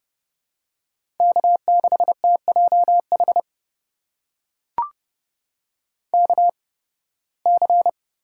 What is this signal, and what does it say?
Morse code, heard as a keyed tone, about 30 wpm, spaced as for 19 wpm, 700 Hz. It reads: K6TJH KC
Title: USA Call Signs - 8x Character Spacing - 30wpm